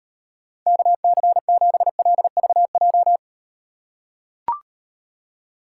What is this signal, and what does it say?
Morse code, heard as a keyed tone, about 38 wpm, 700 Hz. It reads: KC7LVJ